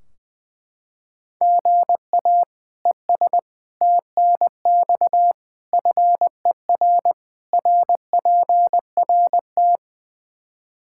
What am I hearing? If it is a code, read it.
Morse code, heard as a keyed tone, about 21 wpm, 700 Hz. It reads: GA ES TNX FER RPRT
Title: K1USN SST QSO Practice 20wpm